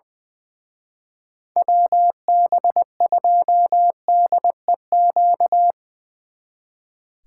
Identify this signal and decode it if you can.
Morse code, heard as a keyed tone, about 23 wpm, 700 Hz. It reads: WB2DEQ
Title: Speed Racing - USA 2x3 Call Signs 20wpm